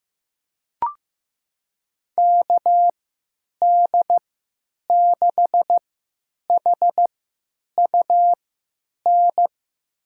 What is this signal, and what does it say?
Morse code, heard as a keyed tone, about 15 wpm, spaced as for 9 wpm, 700 Hz. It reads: KD6HUN
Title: USA Call Signs - 6 Characters - 3x Character Spacing - 15wpm